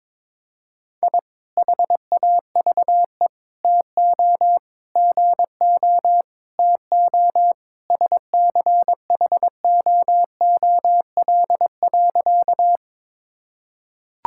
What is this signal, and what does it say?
Morse code, heard as a keyed tone, about 22 wpm, 700 Hz. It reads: I HAVE TO GO TO SCHOOL.
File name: practice / 22wpm / Sentences with 3-Word Phrases - 22wpm